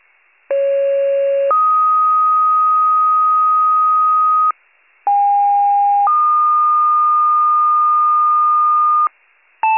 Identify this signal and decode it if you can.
Two-tone page, heard as a one-second tone, then a three-second tone, about 550 then 1250 Hz, about 810 then 1220 Hz. It is heard from 0.5 to 9.1 s.